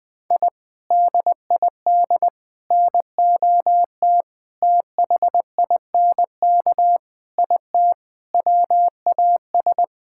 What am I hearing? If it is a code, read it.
Morse code, heard as a keyed tone, about 21 wpm, 700 Hz. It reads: I DID NOT THINK IT WAS
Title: Sentences with 4-Word Phrases - 20wpm